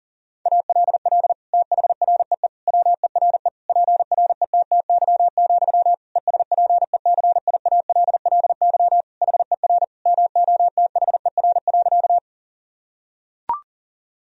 Morse code, heard as a keyed tone, about 37 wpm, 700 Hz. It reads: ALL THREE WERE PRETTY, ESPECIALLY HER MOTHER.
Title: Sentences from Top 1000 Words - Review Part 4 - 40wpm